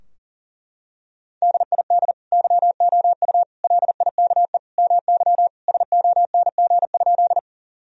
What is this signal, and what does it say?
Morse code, heard as a keyed tone, about 40 wpm, 700 Hz. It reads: DID YOU LIKE MY SONG?
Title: Sentences from Top 1000 Words - Review Part 2 - 40wpm